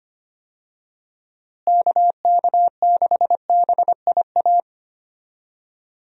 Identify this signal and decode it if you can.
Morse code, heard as a keyed tone, about 25 wpm, 700 Hz. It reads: KK6BIA
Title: USA Call Signs - 6 Characters - Repeated 3x - 25wpm